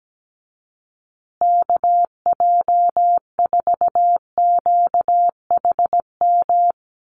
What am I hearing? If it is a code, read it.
Morse code, heard as a keyed tone, about 17 wpm, 700 Hz. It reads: KJ4QHM
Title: USA Call Signs Repeated 2x 17wpm